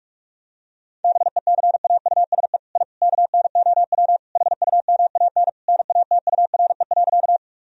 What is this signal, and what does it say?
Morse code, heard as a keyed tone, about 41 wpm, 700 Hz. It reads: BECAUSE I KNOW HUMAN NATURE.
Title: Sentences from Top 1000 Words - Review Part 3 - 45wpm